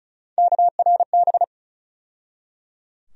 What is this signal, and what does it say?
Morse code, heard as a keyed tone, about 34 wpm, 700 Hz. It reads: KRB
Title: USA Call Sign Postfixes - 3 Letters 35wpm